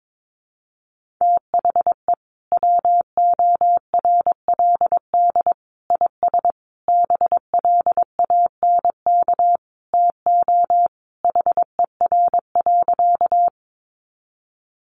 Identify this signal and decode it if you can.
Morse code, heard as a keyed tone, about 22 wpm, 700 Hz. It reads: THE WORLD IS BLANK TO HER.
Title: Sentences from Top 1500 Words 22wpm